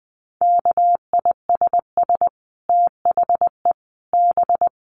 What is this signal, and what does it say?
Morse code, heard as a keyed tone, about 20 wpm, 700 Hz. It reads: KISS THE B
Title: Dit Challenge 20wpm